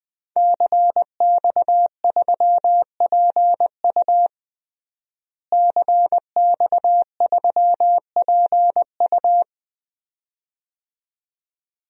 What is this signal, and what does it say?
Morse code, heard as a keyed tone, about 20 wpm, 700 Hz. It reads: CX3PU CX3PU
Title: DXCC-100 Call Signs - 5 Characters - Repeated 2x - 20wpm